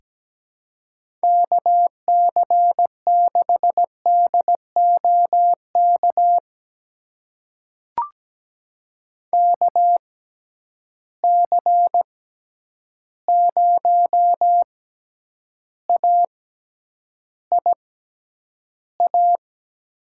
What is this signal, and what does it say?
Morse code, heard as a keyed tone, about 17 wpm, spaced as for 9 wpm, 700 Hz. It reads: KC6DOK KC0AIA
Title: USA Call Signs - 6 Characters - 6x Character Spacing - 17wpm